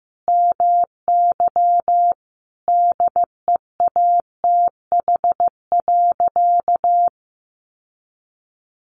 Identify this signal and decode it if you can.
Morse code, heard as a keyed tone, about 15 wpm, 700 Hz. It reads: MY DEATH.
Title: Sentences from Top 1500 Words 15wpm